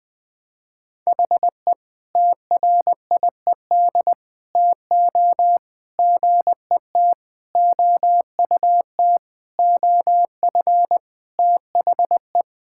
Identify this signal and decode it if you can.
Morse code, heard as a keyed tone, about 21 wpm, 700 Hz. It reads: HE TRIED TO GET OUT OF THE
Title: Sentences with 5-Word Phrases - Spoken Beginning - 20wpm